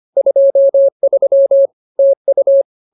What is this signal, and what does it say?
Morse code, heard as a keyed tone, about 25 wpm, 550 Hz. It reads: 23 TU
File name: CWT-209-25